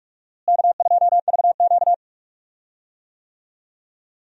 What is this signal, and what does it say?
Morse code, heard as a keyed tone, about 42 wpm, 700 Hz. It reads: K2VQ
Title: USA Call Signs - 4 Characters - Repeated 2x - 45wpm